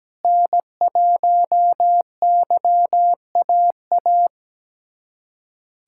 Morse code, heard as a keyed tone, about 17 wpm, 700 Hz. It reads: N1YAA
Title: USA Callsign DE Callsign - 5 Characters - 17wpm